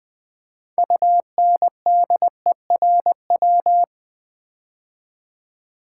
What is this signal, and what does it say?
Morse code, heard as a keyed tone, about 20 wpm, 700 Hz. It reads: UNDERW
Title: Build A Word - 3 Letters At A Time - 9-Letter Words - 20wpm